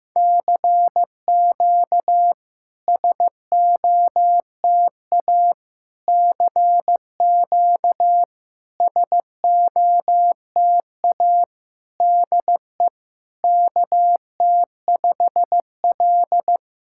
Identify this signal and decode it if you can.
Morse code, heard as a keyed tone, about 15 wpm, 700 Hz. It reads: CQ SOTA CQ SOTA DE KT5L
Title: SOTA QSO Practice 15wpm